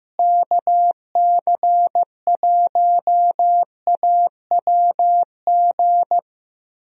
Morse code, heard as a keyed tone, about 15 wpm, 700 Hz. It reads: KC1AWG
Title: USA Call Signs - 6 Characters - Repeated 3x - 15wpm